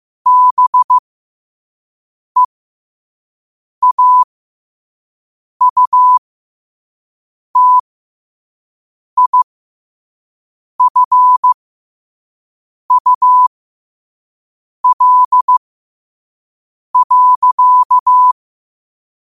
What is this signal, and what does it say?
Morse code, heard as a keyed tone, about 15 wpm, spaced as for 5 wpm, 1000 Hz. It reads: BEAUTIFUL.
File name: qotdfiles / 5wpm